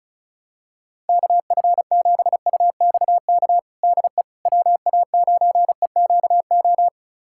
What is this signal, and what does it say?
Morse code, heard as a keyed tone, about 36 wpm, 690 Hz. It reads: KF7UXK DE WA9EQO
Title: USA Callsign DE Callsign 35wpm